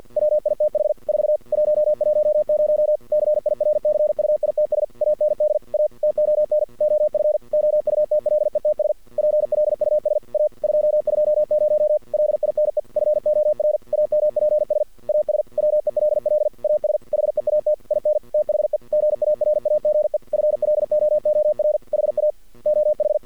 Morse code, heard as a keyed tone, about 50 wpm, 610 Hz. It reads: WITH 4 110 CIRCUITS AND A TON OF OUTLETS PLUS A 220 LINE FOR AMPS AS WELL AS HEAT IN THE GARAGE FLOOR HA OH
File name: archived / CRSnet